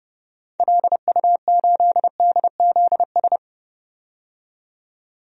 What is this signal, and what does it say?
Morse code, heard as a keyed tone, about 30 wpm, 700 Hz. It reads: LU8DZS
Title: DXCC-100 Call Signs - 6 Characters - 30wpm